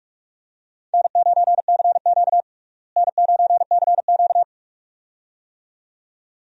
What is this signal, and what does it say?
Morse code, heard as a keyed tone, about 45 wpm, 700 Hz. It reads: N9CQ N9CQ
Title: USA Call Signs - 4 Characters - Repeated 2x - 45wpm